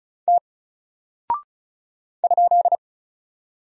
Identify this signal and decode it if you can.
Morse code, heard as a keyed tone, about 36 wpm, 700 Hz. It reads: T ?
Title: Single Character (Letter:Number:Punctuation:Prosign) Alt 35wpm